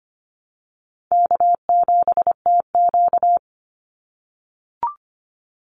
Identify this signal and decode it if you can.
Morse code, heard as a keyed tone, about 25 wpm, 700 Hz. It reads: K7TQ
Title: NA SOTA Chaser Callsigns 25wpm